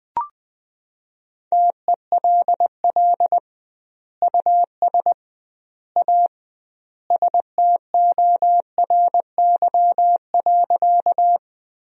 Morse code, heard as a keyed tone, about 20 wpm, 700 Hz. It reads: TELL US A STORY.
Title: 2x Word-Spacing - Sentences from Top 300 Words - 20wpm